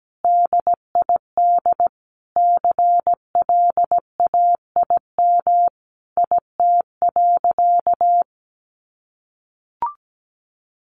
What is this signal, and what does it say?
Morse code, heard as a keyed tone, about 17 wpm, 700 Hz. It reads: DID CLAIM IT.
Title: Sentences from Top 1500 Words 17wpm